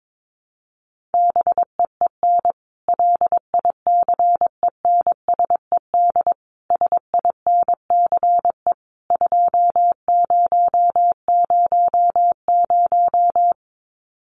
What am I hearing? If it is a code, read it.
Morse code, heard as a keyed tone, about 22 wpm, 700 Hz. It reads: BEEN LICENSED SINCE 2000
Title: Common QSO Phrases 22wpm